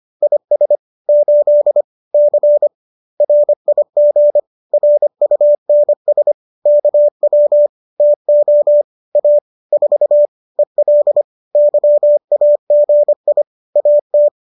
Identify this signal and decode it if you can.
Morse code, heard as a keyed tone, about 25 wpm, 590 Hz. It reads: IS 8 C RIG RUNS KW TO A 4 EL YAGI AT
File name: qso205 25